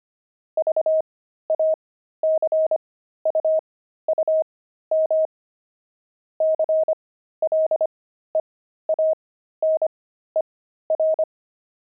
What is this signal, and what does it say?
Morse code, heard as a keyed tone, about 25 wpm, spaced as for 13 wpm, 630 Hz. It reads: VACUUM CLEANER